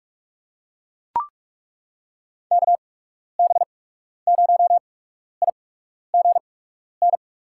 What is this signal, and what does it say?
Morse code, heard as a keyed tone, about 43 wpm, spaced as for 12 wpm, 700 Hz. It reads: KB0IGN
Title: USA Call Signs - 6 Characters - 8x Character Spacing - 45wpm